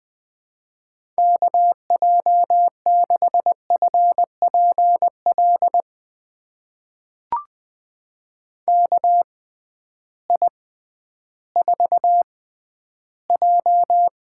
Morse code, heard as a keyed tone, about 20 wpm, spaced as for 12 wpm, 700 Hz. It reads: KJ6FPL KI4J
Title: USA Call Signs - 6 Characters - 6x Character Spacing - 20wpm